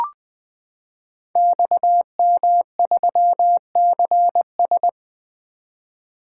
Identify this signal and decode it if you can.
Morse code, heard as a keyed tone, about 20 wpm, 700 Hz. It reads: XM3CS
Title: Code Groups - 5 Letter:Numbers - 20wpm